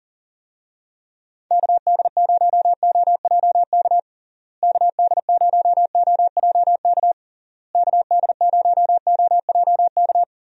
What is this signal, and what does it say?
Morse code, heard as a keyed tone, about 40 wpm, 700 Hz. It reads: KD0OJK KD0OJK KD0OJK
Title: USA Call Signs - 6 Characters - Repeated 3x - 40wpm